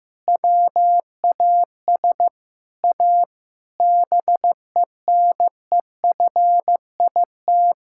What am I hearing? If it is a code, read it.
Morse code, heard as a keyed tone, about 15 wpm, 700 Hz. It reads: WAS A BENEFIT
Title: Sentences from Top 1000 Words - Review Part 6 - 15wpm